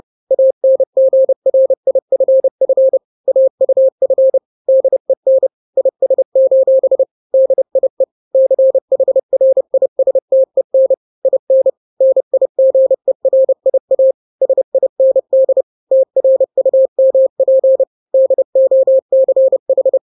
Morse code, heard as a keyed tone, about 30 wpm, 530 Hz. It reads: ANGRIFF AUF DEN IS: DIE CHRISTEN IN NIGERIA SIND TRUMP DOCH
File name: daily / 29wpm